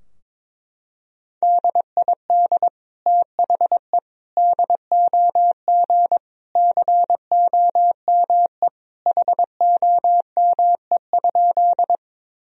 Morse code, heard as a keyed tone, about 22 wpm, 700 Hz. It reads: DID THE DOG COME HOME?
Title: Sentences from Top 1000 Words - Review Part 2 - 22wpm